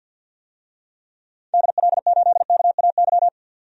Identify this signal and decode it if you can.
Morse code, heard as a keyed tone, about 48 wpm, 700 Hz. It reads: DL8KAY
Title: DXCC-100 Call Signs - 6 Characters - 50wpm